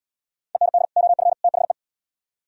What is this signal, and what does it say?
Morse code, heard as a keyed tone, about 75 wpm, 700 Hz. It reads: END OF THE